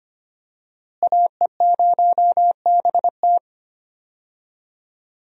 Morse code, heard as a keyed tone, about 25 wpm, 700 Hz. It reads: AE0BT
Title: USA Call Signs - 5 Characters - Repeated 2x - 25wpm